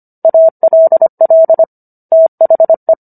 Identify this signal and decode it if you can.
Morse code, heard as a keyed tone, about 25 wpm, 650 Hz. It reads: ALL THE